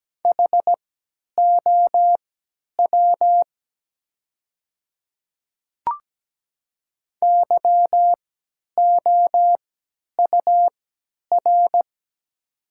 Morse code, heard as a keyed tone, about 17 wpm, spaced as for 10 wpm, 700 Hz. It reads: HOW YOUR
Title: 3x Character Spacing - Top 100 Words - 17wpm